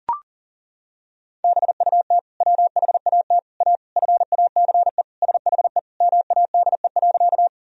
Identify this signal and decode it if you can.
Morse code, heard as a keyed tone, about 40 wpm, 700 Hz. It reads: BUT WHAT A FACE SHE MADE.
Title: Sentences from Top 1000 Words - Review Part 2 - 40wpm